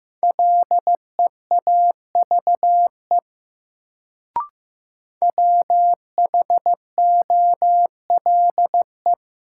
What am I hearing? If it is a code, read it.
Morse code, heard as a keyed tone, about 15 wpm, 700 Hz. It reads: LEAVE WHOLE
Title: Sentences from Top 400 Words - New Word Drill - 15wpm